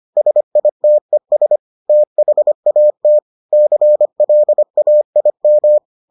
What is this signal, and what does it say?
Morse code, heard as a keyed tone, about 25 wpm, 600 Hz. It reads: SITES THAT CLAIM